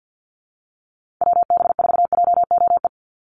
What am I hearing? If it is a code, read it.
Morse code, heard as a keyed tone, about 48 wpm, 700 Hz. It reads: WB4JOE
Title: USA Call Signs 50wpm